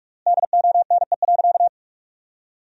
Morse code, heard as a keyed tone, about 46 wpm, 700 Hz. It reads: DONE.
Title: 4x Word-Spacing - Sentences from Top 500 Words - 45wpm